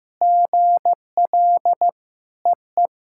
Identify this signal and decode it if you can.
Morse code, heard as a keyed tone, about 15 wpm, 700 Hz. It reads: GL EE
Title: SOTA QSO Practice 15wpm